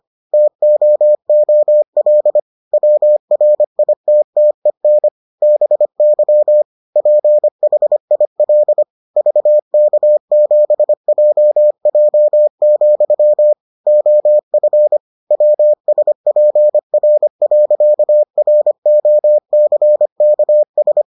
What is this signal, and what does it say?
Morse code, heard as a keyed tone, about 25 wpm, 600 Hz. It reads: TOOL WRITTEN BY PHIL VK7JJ, OF WSPR.ROCKS